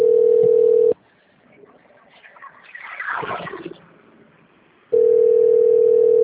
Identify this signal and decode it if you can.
Ringback tone (North America), 440 + 480 Hz.
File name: out- - - - - .67973